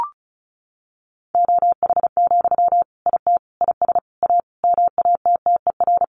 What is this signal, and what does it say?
Morse code, heard as a keyed tone, about 34 wpm, 700 Hz. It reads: OH, IT IS A MATTER
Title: Sentences from Top 1200 Words 35wpm